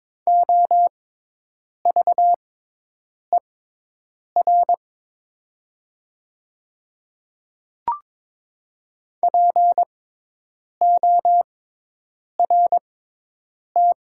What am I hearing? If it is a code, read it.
Morse code, heard as a keyed tone, about 22 wpm, spaced as for 8 wpm, 700 Hz. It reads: OVER PORT
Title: 6x Character Spacing - Top 200 Words - 22wpm